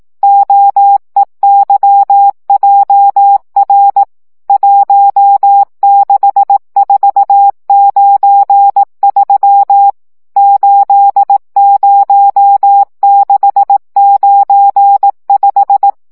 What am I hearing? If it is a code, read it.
Morse code, heard as a keyed tone, about 18 wpm, 800 Hz. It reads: OEYJR 16493 80695
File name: morze vizsga